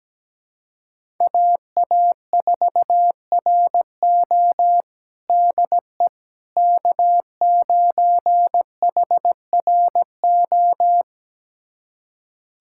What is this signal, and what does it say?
Morse code, heard as a keyed tone, about 17 wpm, 700 Hz. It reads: AA4RO DE K9HRO
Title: USA Callsign DE Callsign - 5 Characters - 17wpm